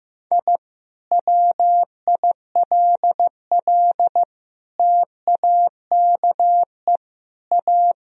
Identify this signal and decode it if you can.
Morse code, heard as a keyed tone, about 15 wpm, 700 Hz. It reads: I WILL TAKE A
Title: Sentences from Top 1000 Words - Review Part 1 - 15wpm